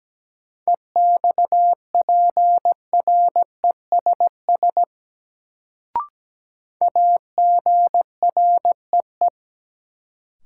Morse code, heard as a keyed tone, about 17 wpm, 700 Hz. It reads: EXPRESS AGREE
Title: Sentences from Top 1100 Words - New Word Drill Alt - 17wpm